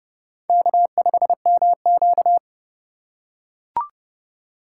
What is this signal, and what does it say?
Morse code, heard as a keyed tone, about 31 wpm, 700 Hz. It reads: K5MQ
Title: USA Call Signs - 2x Character Spacing - 30wpm